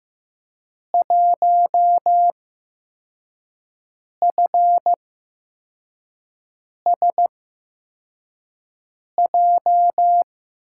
Morse code, heard as a keyed tone, about 15 wpm, spaced as for 4 wpm, 700 Hz. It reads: 1FSJ
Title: USA Call Signs - 6 Characters - 8x Character Spacing - 15wpm